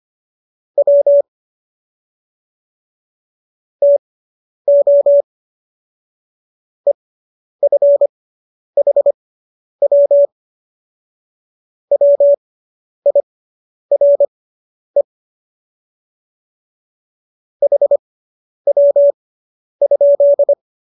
Morse code, heard as a keyed tone, about 25 wpm, spaced as for 10 wpm, 570 Hz. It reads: W TO EFHW WIRE HW?